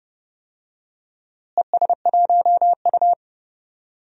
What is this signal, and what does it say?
Morse code, heard as a keyed tone, about 30 wpm, 700 Hz. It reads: ES1U
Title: DXCC-100 Call Signs - 4 Characters - Repeated 3x - 30wpm